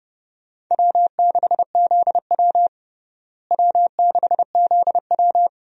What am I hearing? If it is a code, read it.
Morse code, heard as a keyed tone, about 32 wpm, 700 Hz. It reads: W6ZW W6ZW
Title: USA Call Signs - 4 Characters - Repeated 2x - 30wpm